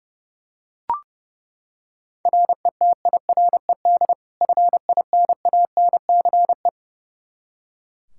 Morse code, heard as a keyed tone, about 30 wpm, 700 Hz. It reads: RETIRED FINANCE
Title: Practice QSO - Exchange 3 - More Info - 30wpm